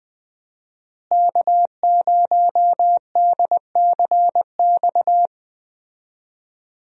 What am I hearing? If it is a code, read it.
Morse code, heard as a keyed tone, about 20 wpm, 700 Hz. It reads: K0DCX
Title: USA Call Signs - 5 Characters - Repeated 2x - 20wpm